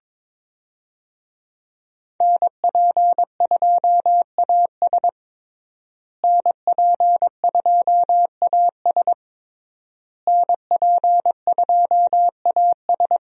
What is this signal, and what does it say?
Morse code, heard as a keyed tone, about 21 wpm, 700 Hz. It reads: NP2AS NP2AS NP2AS
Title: USA Call Signs - 5 Characters - Repeated 3x - 22wpm